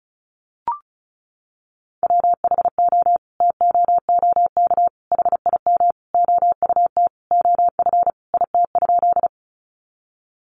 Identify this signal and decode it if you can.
Morse code, heard as a keyed tone, about 34 wpm, 700 Hz. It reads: WHO TOOK HIM OUT OF IT?
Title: Sentences from Top 300 Words 35wpm